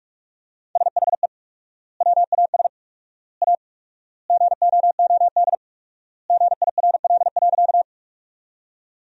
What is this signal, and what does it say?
Morse code, heard as a keyed tone, about 46 wpm, 700 Hz. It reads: SHE WAS A GOOD GIRL.
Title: 4x Word-Spacing - Sentences from Top 400 Words - 45wpm